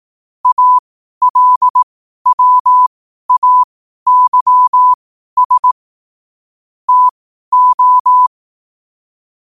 Morse code, heard as a keyed tone, about 18 wpm, spaced as for 12 wpm, 1000 Hz. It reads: ALWAYS TO
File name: qotdfiles / 12wpm